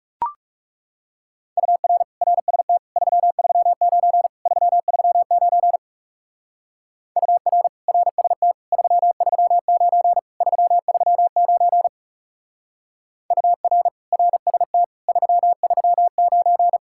Morse code, heard as a keyed tone, about 40 wpm, 700 Hz. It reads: UR RST 339 339 UR RST 339 339 UR RST 339